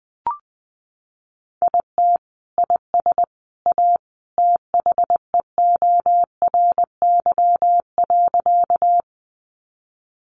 Morse code, heard as a keyed tone, about 20 wpm, 700 Hz. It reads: IT IS A THEORY.
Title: Sentences from Top 600 Words 20wpm